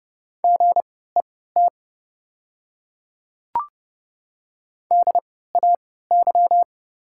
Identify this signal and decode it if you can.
Morse code, heard as a keyed tone, about 30 wpm, spaced as for 17 wpm, 700 Hz. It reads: GET DAY
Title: 3x Character Spacing - Top 200 Words - 30wpm